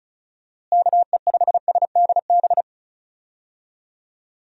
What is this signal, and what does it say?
Morse code, heard as a keyed tone, about 34 wpm, 700 Hz. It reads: KE5SDB
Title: USA Call Signs - 6 Characters - Repeated 2x - 35wpm